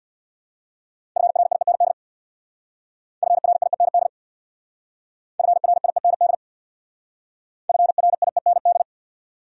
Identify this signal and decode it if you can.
Morse code, heard as a keyed tone, about 66 wpm, 700 Hz. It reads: FRIEND FRIEND FRIEND FRIEND